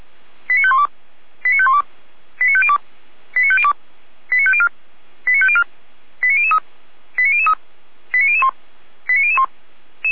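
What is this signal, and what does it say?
Five-tone selective call (ZVEI), heard as quick runs of five tones, tones of about 70 ms. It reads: 89512 89512 89502 89502 89504 89504 89003 89003 89001 89001
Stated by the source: ZVEI five-tone call received on an SDR, channel 455.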